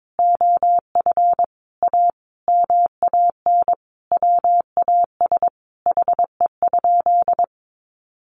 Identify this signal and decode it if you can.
Morse code, heard as a keyed tone, about 22 wpm, 700 Hz. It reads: OF A MAN WAS HE?
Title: Sentences from Top 1300 Words 22wpm